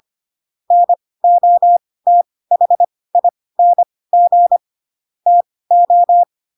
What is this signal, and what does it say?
Morse code, heard as a keyed tone, about 25 wpm, spaced as for 18 wpm, 700 Hz. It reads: NOTHING TO